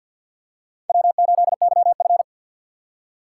Morse code, heard as a keyed tone, about 47 wpm, 700 Hz. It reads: W8YF
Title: USA Call Signs - 4 Characters - 6x Character Spacing - 50wpm